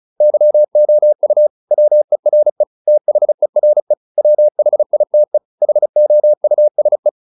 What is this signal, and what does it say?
Morse code, heard as a keyed tone, about 34 wpm, 600 Hz. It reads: YOU WERE THERE WHITE HOUSE